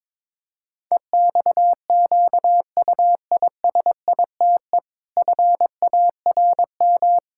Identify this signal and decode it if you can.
Morse code, heard as a keyed tone, about 22 wpm, 700 Hz. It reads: EXQUISITE FARM
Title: Pangram Challenge v2 - 22wpm